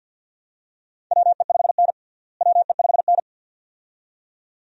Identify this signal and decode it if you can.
Morse code, heard as a keyed tone, about 52 wpm, 700 Hz. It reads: WE5N WE5N
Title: USA Call Signs - 4 Characters - Repeated 2x - 50wpm